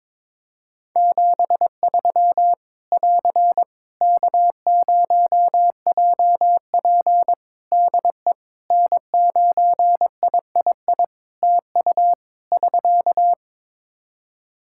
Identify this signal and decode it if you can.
Morse code, heard as a keyed tone, about 22 wpm, 700 Hz. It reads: 73 <AR> K0JP DE N9III TU <SK>
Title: Practice QSO - Exchange 4 - Ending - 22wpm